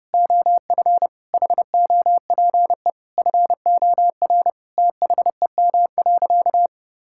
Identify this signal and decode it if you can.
Morse code, heard as a keyed tone, about 32 wpm, 700 Hz. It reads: OF HOPE FOR THEM.
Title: Sentences from Top 1000 Words - Review Part 5 - 30wpm